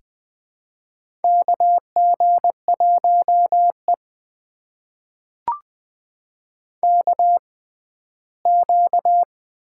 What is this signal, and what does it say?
Morse code, heard as a keyed tone, about 20 wpm, spaced as for 14 wpm, 700 Hz. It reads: KG1E KQ
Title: USA Call Signs - 4 Characters - 6x Character Spacing - 20wpm